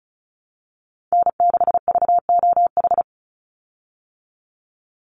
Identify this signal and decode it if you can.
Morse code, heard as a keyed tone, about 34 wpm, 700 Hz. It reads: N6VOH
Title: Call Signs Repeated 3x - 35wpm